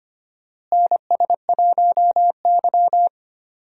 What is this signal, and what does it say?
Morse code, heard as a keyed tone, about 25 wpm, 700 Hz. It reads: NS1Y
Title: USA Call Signs - 4 Characters - Repeated 3x - 25wpm